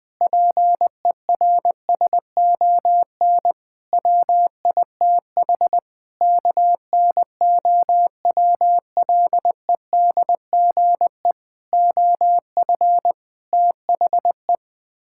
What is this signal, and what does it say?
Morse code, heard as a keyed tone, about 20 wpm, 700 Hz. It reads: PERSON WITH KNOWLEDGE OF THE
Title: Sets of 8 Words - 20wpm